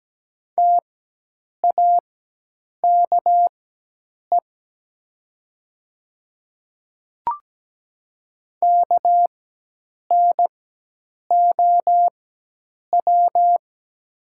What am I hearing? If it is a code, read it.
Morse code, heard as a keyed tone, about 17 wpm, spaced as for 8 wpm, 700 Hz. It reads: TAKE KNOW
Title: 4x Character Spacing - Top 300 Words - 17wpm